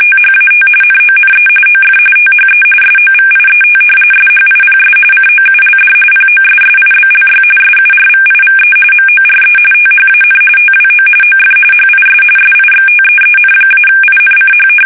RTTY, about 45.45 baud. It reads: THAT NONE OF THE SOVIET SHIPS COULD POSSIBLY SURVIVE MORE THAN TWO OF HIS SIXTEEN-INCH
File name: fsk850 / battleshipnj